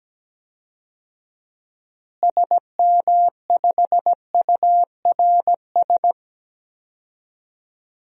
Morse code, heard as a keyed tone, about 17 wpm, 700 Hz. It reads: SM5URS
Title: DXCC-100 Call Signs - 6 Characters - Repeated 3x - 17wpm